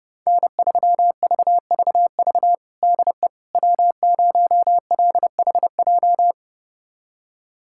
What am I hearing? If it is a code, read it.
Morse code, heard as a keyed tone, about 30 wpm, 700 Hz. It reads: N3VVV DE W0LHJ
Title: USA Callsign DE Callsign - 5 Characters - 30wpm